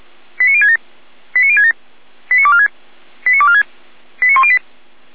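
Five-tone selective call (ZVEI), heard as quick runs of five tones, tones of about 70 ms. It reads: 89067 89067 89246 89246 89108
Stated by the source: Five-tone ZVEI selective call received on an SDR.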